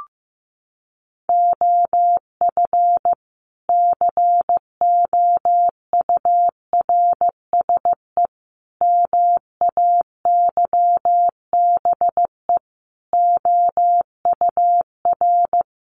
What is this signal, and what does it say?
Morse code, heard as a keyed tone, about 14 wpm, 700 Hz. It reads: OF COURSE MAYBE OUR
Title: A Sound Of Thunder 15wpm